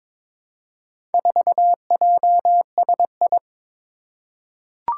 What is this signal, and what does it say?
Morse code, heard as a keyed tone, about 22 wpm, 700 Hz. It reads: 4JSI
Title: Four Challenging Letter-Number Combinations 22wpm